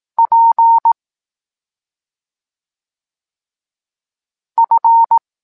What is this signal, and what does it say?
Morse code, heard as a keyed tone, about 18 wpm, 920 Hz. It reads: P F
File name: Tekst-26 1